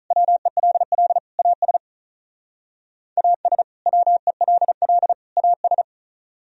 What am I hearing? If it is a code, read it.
Morse code, heard as a keyed tone, about 44 wpm, 700 Hz. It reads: WELL AS AS WELL AS